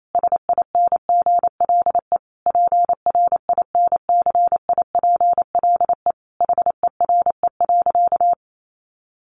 Morse code, heard as a keyed tone, about 28 wpm, 700 Hz. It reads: SINGLE PRINCIPLE HERE.
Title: Sentences from Top 1400 Words 28wpm